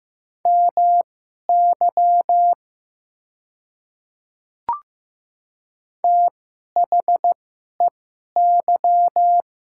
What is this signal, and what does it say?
Morse code, heard as a keyed tone, about 15 wpm, spaced as for 11 wpm, 700 Hz. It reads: MY THEY
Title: 2x Character Spacing - Top 100 Words - 15wpm